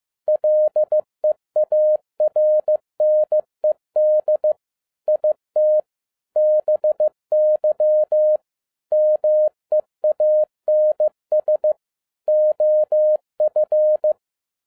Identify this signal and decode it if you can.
Morse code, heard as a keyed tone, about 15 wpm, 610 Hz. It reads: LEARNED IT BY MEANS OF
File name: MY CW LIFE 1 15wpm